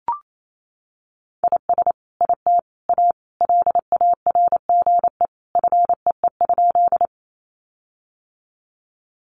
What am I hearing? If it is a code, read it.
Morse code, heard as a keyed tone, about 28 wpm, 700 Hz. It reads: IS IT A LARGE FEE?
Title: Sentences from Top 1500 Words 28wpm